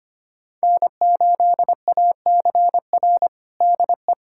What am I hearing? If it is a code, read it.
Morse code, heard as a keyed tone, about 25 wpm, 700 Hz. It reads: N8ACR DE
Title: USA Callsign DE Callsign 25wpm